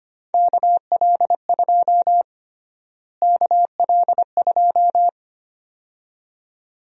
Morse code, heard as a keyed tone, about 25 wpm, 700 Hz. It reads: KL2 KL2
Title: USA Call Sign Prefixes - Repeated 2x - 25wpm